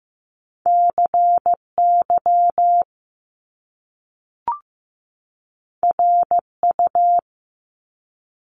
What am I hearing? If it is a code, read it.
Morse code, heard as a keyed tone, about 15 wpm, 700 Hz. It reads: CY RU
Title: Top Two Letter Combinations in Words 15wpm